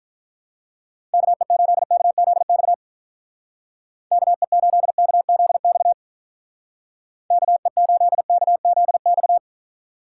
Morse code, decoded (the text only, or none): KE8KZX KE8KZX KE8KZX